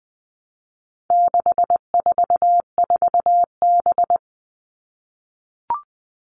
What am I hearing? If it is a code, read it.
Morse code, heard as a keyed tone, about 20 wpm, 700 Hz. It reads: 644B
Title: All Cut Numbers - 4-Digit 20wpm